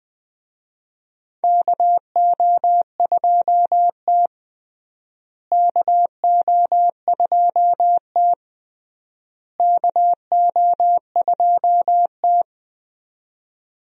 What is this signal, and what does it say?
Morse code, heard as a keyed tone, about 20 wpm, 700 Hz. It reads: KO2T KO2T KO2T
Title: USA Call Signs - 4 Characters - Repeated 3x - 20wpm